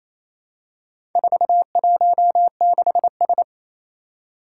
Morse code, heard as a keyed tone, about 27 wpm, 700 Hz. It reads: 416S
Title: Four Challenging Letter-Number Combinations 28wpm